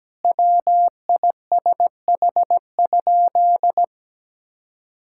Morse code, heard as a keyed tone, about 17 wpm, 700 Hz. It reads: WISH?
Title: Sentences from Top 1600 Words 17wpm